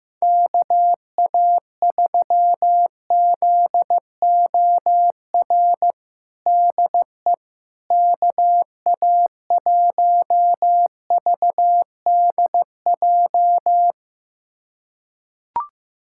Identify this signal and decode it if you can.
Morse code, heard as a keyed tone, about 15 wpm, 700 Hz. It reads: KA3ZOR DE KA1VDJ
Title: USA Callsign DE Callsign - 6 Characters - 15wpm